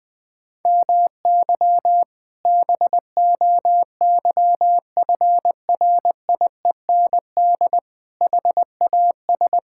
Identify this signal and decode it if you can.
Morse code, heard as a keyed tone, about 19 wpm, 700 Hz. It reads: MY BOYFRIEND HAS
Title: Sentences from Top 1000 Words - Review Part 5 - 20wpm